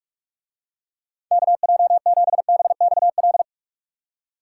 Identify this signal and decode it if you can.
Morse code, heard as a keyed tone, about 43 wpm, 700 Hz. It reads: KJ7BXL